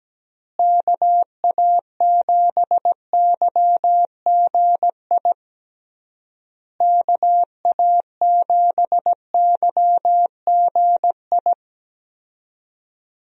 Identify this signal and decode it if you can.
Morse code, heard as a keyed tone, about 17 wpm, 700 Hz. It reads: KA7YGI KA7YGI
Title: USA Call Signs - 6 Characters - Repeated 3x - 17wpm